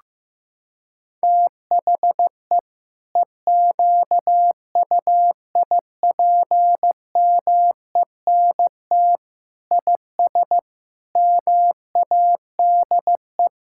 Morse code, heard as a keyed tone, about 15 wpm, 700 Hz. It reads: THE EQUIPMENT IS MADE
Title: Sentences from Top 1000 Words - Review Part 6 - 15wpm